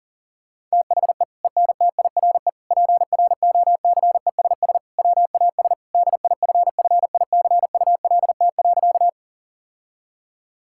Morse code, heard as a keyed tone, about 40 wpm, 700 Hz. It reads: THE ENTIRE PROCESS WAS DIFFICULT.
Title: Sentences from Top 1000 Words - Review Part 5 - 40wpm